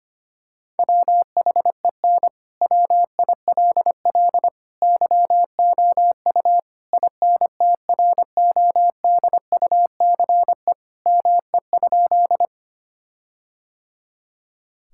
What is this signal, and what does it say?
Morse code, heard as a keyed tone, about 25 wpm, 700 Hz. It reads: WHEN WILL YOU INTRODUCE ME?
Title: Sentences from Top 1600 Words 25wpm